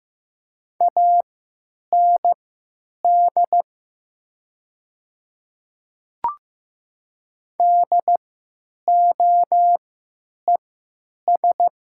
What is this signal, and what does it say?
Morse code, heard as a keyed tone, about 15 wpm, spaced as for 8 wpm, 700 Hz. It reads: AND DOES
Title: 3x Character Spacing - Top 200 Words - 15wpm